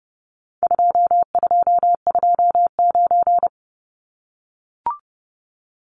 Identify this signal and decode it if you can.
Morse code, heard as a keyed tone, about 30 wpm, 700 Hz. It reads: 2229
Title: Numbers - 4-Digit 30wpm